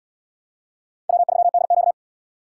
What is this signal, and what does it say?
Morse code, heard as a keyed tone, about 78 wpm, 700 Hz. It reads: W3DQ